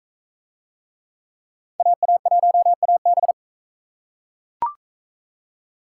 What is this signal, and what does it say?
Morse code, heard as a keyed tone, about 43 wpm, 700 Hz. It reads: AA1AB